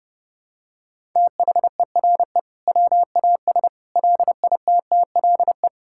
Morse code, heard as a keyed tone, about 30 wpm, 700 Hz. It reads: THERE WAS LITTLE
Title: Sentences from Top 1000 Words - Review Part 6 - 30wpm